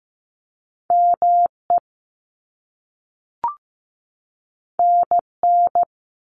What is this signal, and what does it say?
Morse code, heard as a keyed tone, about 15 wpm, 700 Hz. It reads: ME NN
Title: Top Two Letter Combinations in Words 15wpm